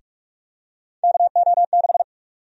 Morse code, heard as a keyed tone, about 46 wpm, 700 Hz. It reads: KO6
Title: USA Call Sign Prefixes - 45wpm